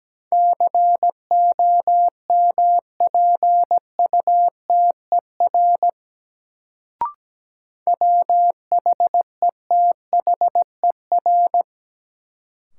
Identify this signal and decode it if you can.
Morse code, heard as a keyed tone, about 17 wpm, 700 Hz. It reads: COMPUTER WHETHER
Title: Sentences from Top 1000 Words - Review Part 3 - New Word Drill - 17wpm